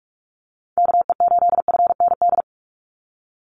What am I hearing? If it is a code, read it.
Morse code, heard as a keyed tone, about 44 wpm, 700 Hz. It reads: KE8FND
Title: USA Call Signs 45wpm